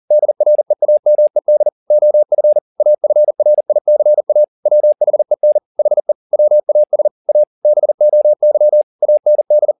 Morse code, heard as a keyed tone, about 40 wpm, 600 Hz. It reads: DREAMED OF AFRICA WHEN HE WAS A BOY AND